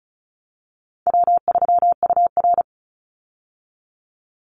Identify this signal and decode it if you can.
Morse code, heard as a keyed tone, about 34 wpm, 700 Hz. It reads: W3UR
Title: Call Signs Repeated 3x - 35wpm